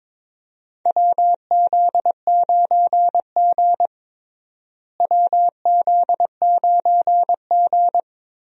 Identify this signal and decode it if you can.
Morse code, heard as a keyed tone, about 22 wpm, 700 Hz. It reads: WZ9G WZ9G
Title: USA Call Signs - 4 Characters - Repeated 3x - 22wpm